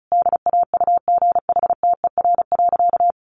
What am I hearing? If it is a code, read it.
Morse code, heard as a keyed tone, about 34 wpm, 700 Hz. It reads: DAUGHTER.
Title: Sentences from Top 1500 Words 35wpm